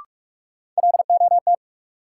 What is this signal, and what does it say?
Morse code, heard as a keyed tone, about 46 wpm, 700 Hz. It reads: LOT
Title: Sentences from Top 500 Words - New Word Drill - 45wpm